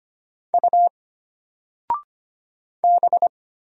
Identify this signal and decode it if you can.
Morse code, heard as a keyed tone, about 25 wpm, 700 Hz. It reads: U B
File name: Single-Letter-Number-25wpm